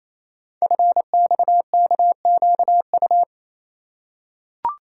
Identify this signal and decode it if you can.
Morse code, heard as a keyed tone, about 27 wpm, 700 Hz. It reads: FXKQU
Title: Code Groups - 5 Letters - 28wpm